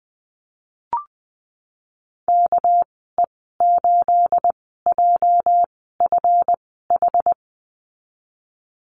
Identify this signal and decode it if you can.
Morse code, heard as a keyed tone, about 20 wpm, 700 Hz. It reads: K E 8 J F H
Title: 238 Lesson 38g - Letter q - Call Signs 20wpm